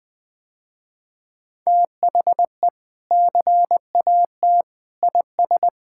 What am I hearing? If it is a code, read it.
Morse code, heard as a keyed tone, about 20 wpm, 700 Hz. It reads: THE CAT IS
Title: Sentences with 2-Word Phrases - Spoken Beginning - 20wpm